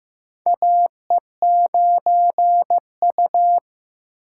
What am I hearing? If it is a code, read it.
Morse code, heard as a keyed tone, about 15 wpm, 700 Hz. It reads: AE9U
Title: USA Call Signs - 4 Characters - 2x Character Spacing - 15wpm